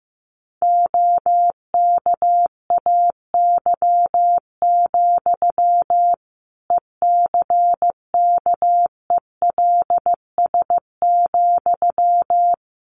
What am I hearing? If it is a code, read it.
Morse code, heard as a keyed tone, about 14 wpm, 700 Hz. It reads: OKAY, ECKELS,
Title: A Sound Of Thunder 15wpm